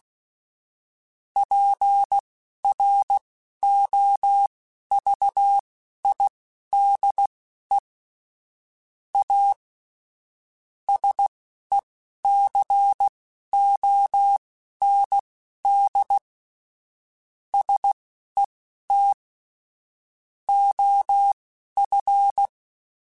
Morse code, heard as a keyed tone, about 16 wpm, spaced as for 11 wpm, 800 Hz. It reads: PROVIDE A SECOND SET OF
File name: Technical Subjects / Technical Subjects-mp3s / Sw Pwr Sup @10